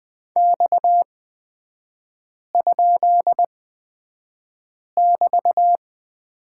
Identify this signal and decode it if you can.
Morse code, heard as a keyed tone, about 20 wpm, spaced as for 5 wpm, 690 Hz. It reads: X?<BT>
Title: 229 Lesson 37e - Letter x - Three Characters 20wpm